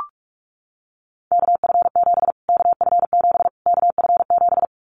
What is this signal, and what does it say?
Morse code, heard as a keyed tone, about 44 wpm, 700 Hz. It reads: KF7 KF7 KF7
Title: Callsign Prefixes for Top 25 DXCC - Repeated 3x - 45wpm